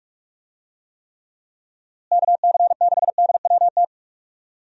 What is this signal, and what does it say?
Morse code, heard as a keyed tone, about 46 wpm, 700 Hz. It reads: KC6DWT
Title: USA Call Signs - 6 Characters - Repeated 2x - 45wpm